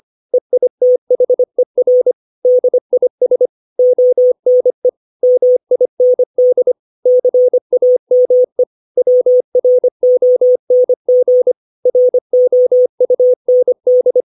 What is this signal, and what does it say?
Morse code, heard as a keyed tone, about 25 wpm, 500 Hz. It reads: EITHER DIS ONE MIND CAME WRONG ROUND
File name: 632 words / 632 Words 25WPM